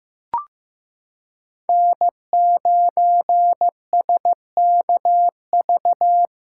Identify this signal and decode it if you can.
Morse code, heard as a keyed tone, about 16 wpm, 700 Hz. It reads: N9SKV
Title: USA Callsign DE Callsign - 5 Characters - 15wpm